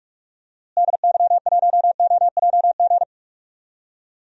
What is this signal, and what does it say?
Morse code, heard as a keyed tone, about 47 wpm, 700 Hz. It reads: DY1OJG
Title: DXCC-100 Call Signs - 6 Characters - Repeated 2x - 45wpm